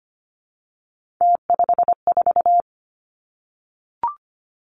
Morse code, heard as a keyed tone, about 25 wpm, 700 Hz. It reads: T54
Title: Three Challenging Letter-Number Combinations 25wpm